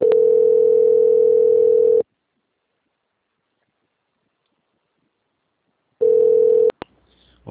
Ringback tone (North America), 440 + 480 Hz.